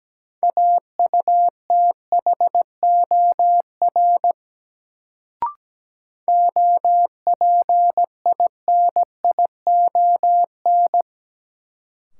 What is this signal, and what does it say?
Morse code, heard as a keyed tone, about 17 wpm, 700 Hz. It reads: AUTHOR OPINION
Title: Sentences from Top 1200 Words - New Word Drill Alt - 17wpm